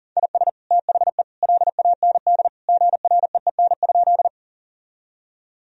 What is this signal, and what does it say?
Morse code, heard as a keyed tone, about 40 wpm, 700 Hz. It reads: IS THE LAND GREEN?
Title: Sentences from Top 1000 Words - Review Part 2 - 40wpm